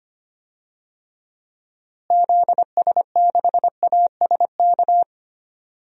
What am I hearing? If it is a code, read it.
Morse code, heard as a keyed tone, about 25 wpm, 700 Hz. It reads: ZS6ASK
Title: DXCC-100 Call Signs - 6 Characters - Repeated 3x - 25wpm